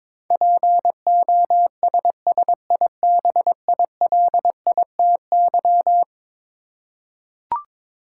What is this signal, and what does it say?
Morse code, heard as a keyed tone, about 22 wpm, 700 Hz. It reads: POSSIBILITY
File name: practice / 22wpm / Eleven Letter Words - 22wpm